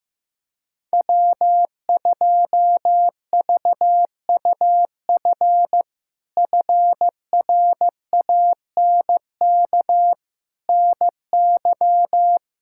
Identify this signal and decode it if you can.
Morse code, heard as a keyed tone, about 15 wpm, 700 Hz. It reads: W2VUF FRANK NY
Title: K1USN SST QSO Practice - 15wpm